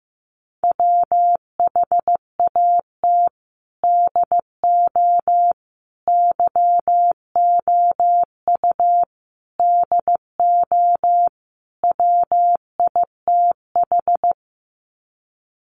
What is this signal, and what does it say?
Morse code, heard as a keyed tone, about 15 wpm, 700 Hz. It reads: WHAT DO YOU DO WITH
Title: Sets of 5 Words Encore2 15wpm